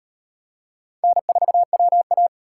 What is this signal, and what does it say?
Morse code, heard as a keyed tone, about 35 wpm, 700 Hz. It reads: N4WA